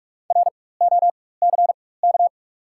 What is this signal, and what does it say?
Morse code, heard as a keyed tone, about 44 wpm, spaced as for 21 wpm, 700 Hz. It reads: ROCK
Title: 4x Character Spacing - Top 500 Words - 45wpm